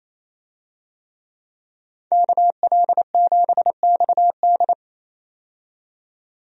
Morse code, heard as a keyed tone, about 28 wpm, 700 Hz. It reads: KL7XD
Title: DXCC-100 Call Signs - 5 Characters - 28wpm